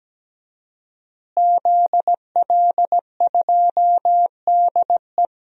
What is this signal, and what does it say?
Morse code, heard as a keyed tone, about 17 wpm, 700 Hz. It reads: ZL2DE
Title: DXCC-100 Call Signs - 5 Characters - Repeated 3x - 17wpm